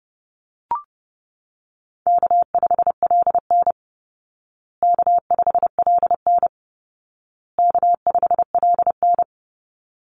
Morse code, heard as a keyed tone, about 28 wpm, 700 Hz. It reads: K5LN K5LN K5LN
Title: Easy Call Signs Repeated 3x - 30wpm